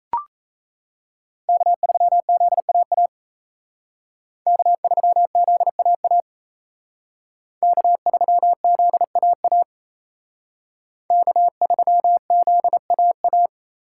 Morse code, heard as a keyed tone, about 32 wpm, 700 Hz. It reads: K3ZAA K3ZAA K3ZAA K3ZAA
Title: Speed Racing - USA 1x3 Call Signs 28wpm